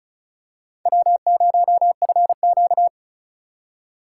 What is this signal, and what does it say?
Morse code, heard as a keyed tone, about 34 wpm, 700 Hz. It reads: W0FQ
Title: USA Call Signs - 2x Character Spacing - 35wpm